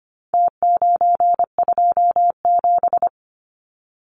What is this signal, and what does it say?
Morse code, heard as a keyed tone, about 25 wpm, 700 Hz. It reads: T927
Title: Numbers - Contest Style - 4-Digit 25wpm